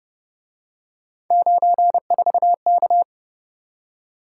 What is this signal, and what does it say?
Morse code, heard as a keyed tone, about 29 wpm, 700 Hz. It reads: 94K